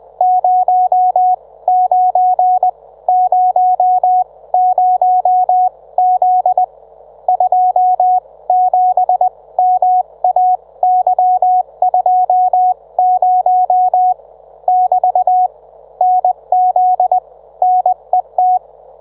Morse code, heard as a keyed tone, about 20 wpm, spaced as for 17 wpm, 730 Hz. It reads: 09 00Z 27MAY20 <BT> NZ NET